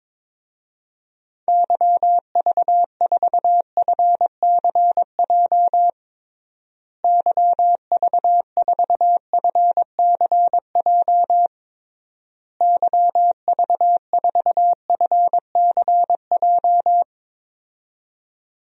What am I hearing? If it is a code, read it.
Morse code, heard as a keyed tone, about 22 wpm, 700 Hz. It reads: YV4FCJ YV4FCJ YV4FCJ
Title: DXCC-100 Call Signs - 6 Characters - Repeated 3x - 22wpm